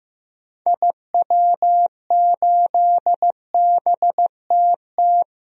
Morse code, heard as a keyed tone, about 15 wpm, 700 Hz. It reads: IW8BTT
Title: DXCC-100 Call Signs - 6 Characters - Repeated 3x - 15wpm